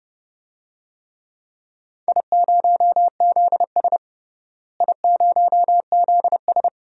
Morse code, heard as a keyed tone, about 30 wpm, 700 Hz. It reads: I0ZS I0ZS
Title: DXCC-100 Call Signs - 4 Characters - Repeated 3x - 30wpm